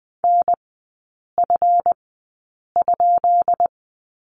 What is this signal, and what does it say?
Morse code, heard as a keyed tone, about 20 wpm, spaced as for 8 wpm, 700 Hz. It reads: NF?
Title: 138 Lesson 23e - Letter f - Three Characters 20wpm